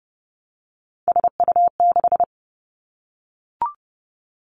Morse code, heard as a keyed tone, about 32 wpm, 700 Hz. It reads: SU6
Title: Three Challenging Letter-Number Combinations 30wpm